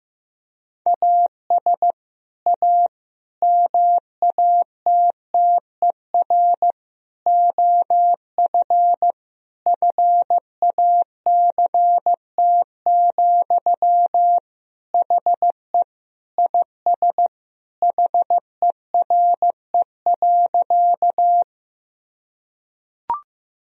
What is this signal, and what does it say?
Morse code, heard as a keyed tone, about 15 wpm, 700 Hz. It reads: AS A MATTER OF FACT, HE IS HERE.
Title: Sentences from Top 1000 Words - Review Part 3 - 15wpm